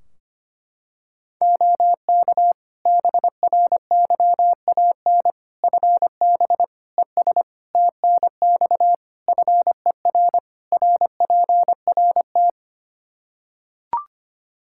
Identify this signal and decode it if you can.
Morse code, heard as a keyed tone, about 25 wpm, 700 Hz. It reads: OK BRYAN FB ES TNX FER RPRT
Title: Practice QSO - Exchange 2 - Info - 25wpm